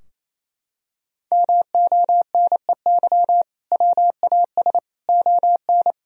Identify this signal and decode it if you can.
Morse code, heard as a keyed tone, about 27 wpm, 700 Hz. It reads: MONEY WAS ON
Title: Sentences from Top 1000 Words - Review Part 2 - 28wpm